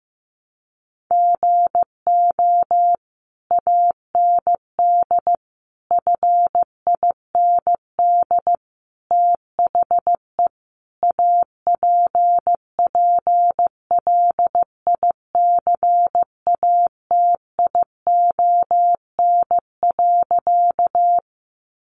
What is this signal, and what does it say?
Morse code, heard as a keyed tone, about 15 wpm, 700 Hz. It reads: GO AND FIND THE APPLICATION.
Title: Sentences from Top 1200 Words 15wpm